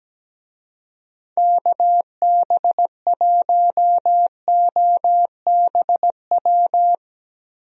Morse code, heard as a keyed tone, about 17 wpm, 700 Hz. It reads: KB1OBW
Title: USA Call Signs - 6 Characters - Repeated 3x - 17wpm